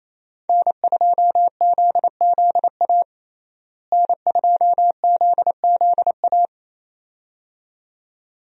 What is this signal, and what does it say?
Morse code, heard as a keyed tone, about 28 wpm, 700 Hz. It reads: N2ZZA N2ZZA
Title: USA Call Signs - 5 Characters - Repeated 2x - 28wpm